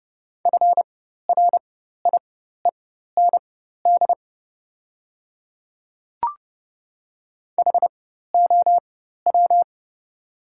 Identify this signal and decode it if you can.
Morse code, heard as a keyed tone, about 30 wpm, spaced as for 14 wpm, 700 Hz. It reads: FRIEND HOW
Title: 4x Character Spacing - Top 300 Words - 30wpm